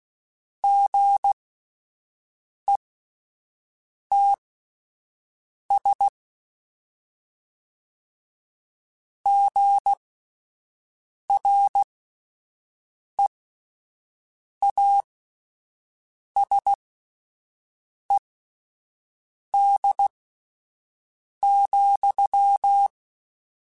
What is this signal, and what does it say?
Morse code, heard as a keyed tone, about 16 wpm, spaced as for 5 wpm, 800 Hz. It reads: GETS GREASED,
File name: Spaced / Chuckles Spaced-mp3s / Squeaky sp @10